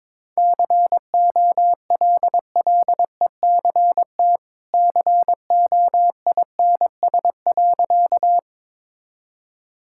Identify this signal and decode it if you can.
Morse code, heard as a keyed tone, about 22 wpm, 700 Hz. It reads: COLLECT COINS.
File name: practice / 22wpm / Easy Sentences with Hints - 22wpm